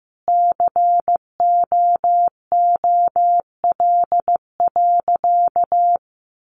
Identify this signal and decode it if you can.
Morse code, heard as a keyed tone, about 15 wpm, 700 Hz. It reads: COOL.
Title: Sentences from Top 1200 Words 15wpm